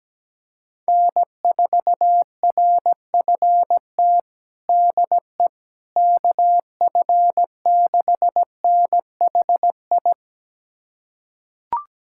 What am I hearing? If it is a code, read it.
Morse code, heard as a keyed tone, about 17 wpm, 700 Hz. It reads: N4RFT DE KF6NHI
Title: USA Callsign DE Callsign 17wpm